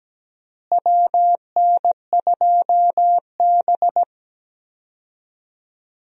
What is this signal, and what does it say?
Morse code, heard as a keyed tone, about 17 wpm, 700 Hz. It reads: WN2B
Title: USA Call Signs - 4 Characters - Repeated 3x - 17wpm